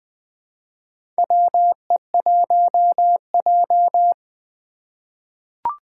Morse code, heard as a keyed tone, about 20 wpm, 700 Hz. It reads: WE1J
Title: USA Call Signs - 4 Characters - 3x Character Spacing - 20wpm